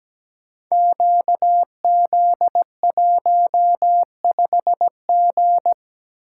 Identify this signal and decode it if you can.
Morse code, heard as a keyed tone, about 17 wpm, 700 Hz. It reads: QZ15G
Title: Code Groups - 5 Letter:Numbers - 17wpm